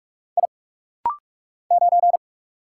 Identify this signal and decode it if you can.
Morse code, heard as a keyed tone, about 44 wpm, 700 Hz. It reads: I 9
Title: Single Letter-Number - Rapid-Fire 45wpm